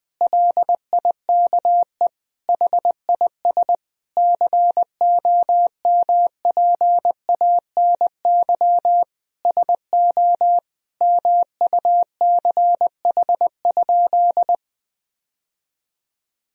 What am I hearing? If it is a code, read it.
Morse code, heard as a keyed tone, about 20 wpm, 700 Hz. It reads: LIKE HIS COMPANY SO MUCH?
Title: Sentences from Top 1000 Words - Review Part 3 - 20wpm